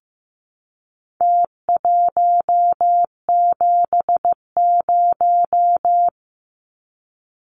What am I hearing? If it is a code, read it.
Morse code, heard as a keyed tone, about 15 wpm, 700 Hz. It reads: T170
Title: Numbers - Contest Style - 4-Digit 15wpm